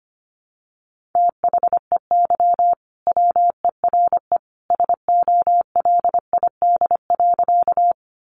Morse code, heard as a keyed tone, about 25 wpm, 700 Hz. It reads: THEY WERE SOLID.
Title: Sentences from Top 1500 Words 25wpm